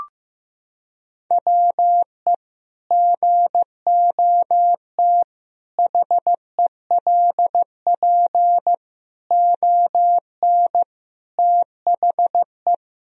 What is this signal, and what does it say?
Morse code, heard as a keyed tone, about 15 wpm, 700 Hz. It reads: WE GOT HELP ON THE
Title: Sentences from Top 1000 Words - Review Part 2 - 15wpm